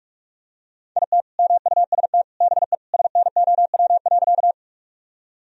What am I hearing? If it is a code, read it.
Morse code, heard as a keyed tone, about 41 wpm, 700 Hz. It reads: IT MUST BE SNOW.
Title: Sentences from Top 1000 Words - Review Part 3 - 45wpm